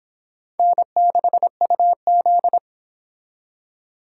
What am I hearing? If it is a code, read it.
Morse code, heard as a keyed tone, about 25 wpm, 700 Hz. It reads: N6UZ